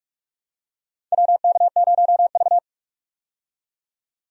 Morse code, heard as a keyed tone, about 46 wpm, 700 Hz. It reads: WK0V
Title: USA Call Signs - 4 Characters - 45wpm